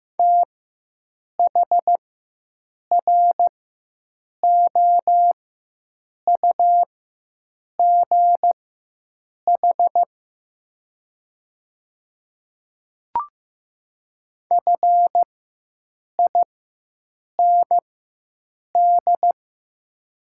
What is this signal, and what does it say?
Morse code, heard as a keyed tone, about 15 wpm, spaced as for 7 wpm, 700 Hz. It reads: THROUGH FIND
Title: 4x Character Spacing - Top 200 Words - 15wpm